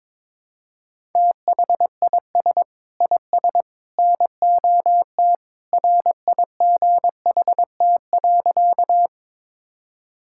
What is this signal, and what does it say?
Morse code, heard as a keyed tone, about 23 wpm, 700 Hz. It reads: THIS IS NOT RIGHT.
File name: practice / 22wpm / Sentences with 3-Word Phrases - 22wpm